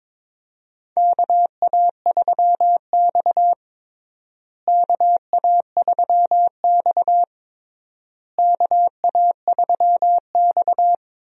Morse code, heard as a keyed tone, about 22 wpm, 700 Hz. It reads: KA3X KA3X KA3X
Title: USA Call Signs - 4 Characters - Repeated 3x - 22wpm